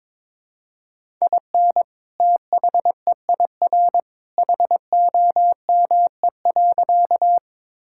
Morse code, heard as a keyed tone, about 22 wpm, 700 Hz. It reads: IN THEIR HOME.
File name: practice / 22wpm / Sentences with 2-Word Phrases - 22wpm